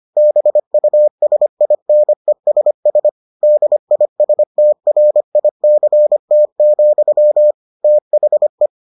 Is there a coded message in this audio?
Yes: Morse code from 0.2 to 8.7 s, heard as a keyed tone, about 25 wpm, 600 Hz. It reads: BUSINESS DISTRICT, THE